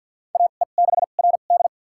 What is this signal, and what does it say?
Morse code, heard as a keyed tone, about 53 wpm, spaced as for 37 wpm, 700 Hz. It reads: AE6RD
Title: USA Call Signs - 5 Characters - 2x Character Spacing - 50wpm